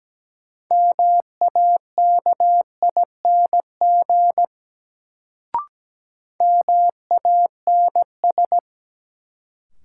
Morse code, heard as a keyed tone, about 17 wpm, 700 Hz. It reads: MAKING MANS
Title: Sentences from Top 1000 Words - Review Part 6 - New Word Drill - 17wpm